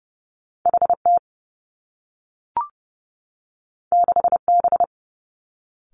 Morse code, heard as a keyed tone, about 30 wpm, 700 Hz. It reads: H T 6 B
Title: Two Challenging Letter-Number Combinations 30wpm